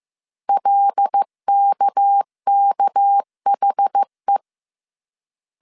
Morse code, heard as a keyed tone, about 15 wpm, 800 Hz. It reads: LKKHE